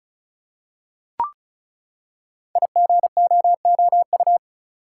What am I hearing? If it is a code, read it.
Morse code, heard as a keyed tone, about 36 wpm, 690 Hz. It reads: IGOOU
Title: Code Groups - 5 Letters - 35wpm